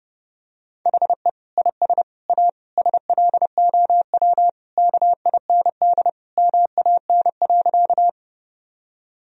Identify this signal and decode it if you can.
Morse code, heard as a keyed tone, about 30 wpm, 700 Hz. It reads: HE IS A SLOW KIND MAN.
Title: Sentences from Top 1000 Words - Review Part 2 - 30wpm